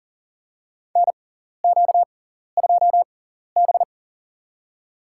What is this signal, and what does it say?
Morse code, heard as a keyed tone, about 40 wpm, spaced as for 14 wpm, 700 Hz. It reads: NQ2B
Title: USA Call Signs - 4 Characters - 6x Character Spacing - 40wpm